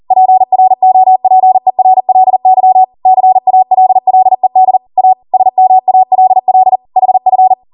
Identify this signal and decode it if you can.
Morse code, heard as a keyed tone, about 40 wpm, 750 Hz. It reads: PROPERLY CALLED A SMALL HF
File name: Morse / 40WPM